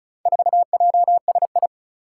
Morse code, heard as a keyed tone, about 36 wpm, 700 Hz. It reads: 4JSI
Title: Four Challenging Letter-Number Combinations 35wpm